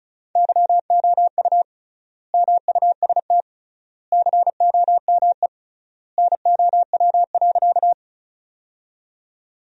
Morse code, heard as a keyed tone, about 36 wpm, 700 Hz. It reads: YOU MUST COME NOW.
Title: 3x Word-Spacing - Sentences from Top 200 Words - 35wpm